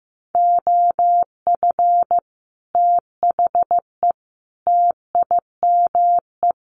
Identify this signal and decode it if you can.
Morse code, heard as a keyed tone, about 15 wpm, 700 Hz. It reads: OF THE TIME
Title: Sets of 3 Words 15wpm